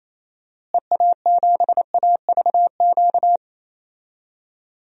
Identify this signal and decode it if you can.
Morse code, heard as a keyed tone, about 28 wpm, 700 Hz. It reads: EA7AVQ
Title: DXCC-100 Call Signs - 6 Characters - Repeated 3x - 28wpm